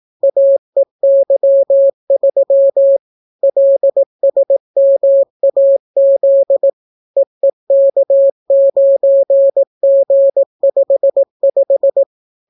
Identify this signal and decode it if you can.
Morse code, heard as a keyed tone, about 18 wpm, 550 Hz. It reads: AEY3 LSMAZ EEK9G55